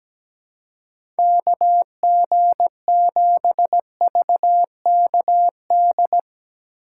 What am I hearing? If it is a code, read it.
Morse code, heard as a keyed tone, about 17 wpm, 700 Hz. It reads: KG7VKD
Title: USA Call Signs - 2x Character Spacing - 17wpm